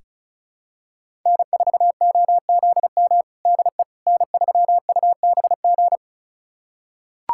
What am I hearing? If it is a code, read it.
Morse code, heard as a keyed tone, about 36 wpm, 700 Hz. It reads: N4OZM DE N3UBG
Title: USA Callsign DE Callsign - 5 Characters - 35wpm